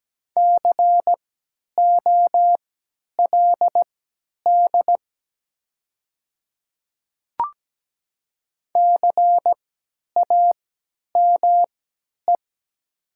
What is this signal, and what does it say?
Morse code, heard as a keyed tone, about 17 wpm, spaced as for 10 wpm, 700 Hz. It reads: COLD CAME
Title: 3x Character Spacing - Top 500 Words - 17wpm